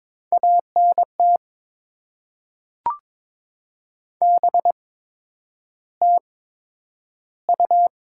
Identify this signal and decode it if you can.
Morse code, heard as a keyed tone, about 22 wpm, spaced as for 9 wpm, 700 Hz. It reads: ANT BTU
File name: Top 20 QSO Elements - 8x Character Spacing - 22wpm